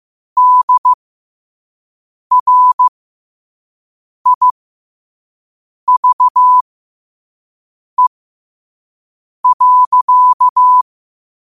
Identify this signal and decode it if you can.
Morse code, heard as a keyed tone, about 15 wpm, spaced as for 5 wpm, 1000 Hz. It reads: DRIVE.
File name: qotdfiles / 5wpm